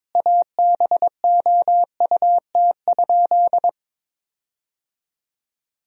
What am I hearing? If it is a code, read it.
Morse code, heard as a keyed tone, about 22 wpm, 700 Hz. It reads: ABOUT?
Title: 4x Word-Spacing - Sentences from Top 100 Words - 22wpm